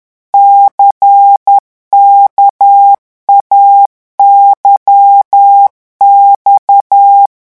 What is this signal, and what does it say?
Morse code, heard as a keyed tone, about 10 wpm, 800 Hz. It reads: CKAYX
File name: morse mp3 / revision2-vitesse 10 mots